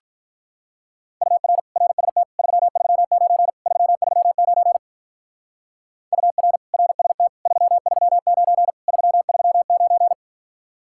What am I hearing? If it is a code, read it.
Morse code, heard as a keyed tone, about 44 wpm, 700 Hz. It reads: UR RST 339 339 UR RST 339 339
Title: Speed-Racing Common QSO Phrases - 35wpm